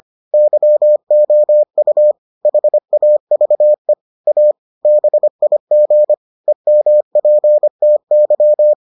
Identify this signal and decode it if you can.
Morse code, heard as a keyed tone, about 25 wpm, 600 Hz. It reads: YOU HAVE A BIG EMPTY